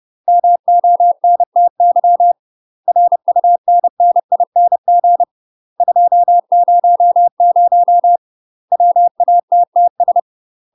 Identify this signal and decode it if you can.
Morse code, heard as a keyed tone, about 30 wpm, 700 Hz. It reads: MONTY RUNNING 200 WATTS
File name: UK MASTER QSOs TRAINING/30WPM